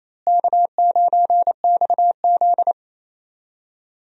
Morse code, heard as a keyed tone, about 27 wpm, 700 Hz. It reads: K9XZ
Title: USA Call Signs - 4 Characters - Repeated 3x - 28wpm